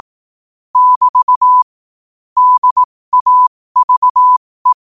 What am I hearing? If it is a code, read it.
Morse code, heard as a keyed tone, about 18 wpm, spaced as for 15 wpm, 1000 Hz. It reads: <BT> DAVE
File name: qotdfiles / 15wpm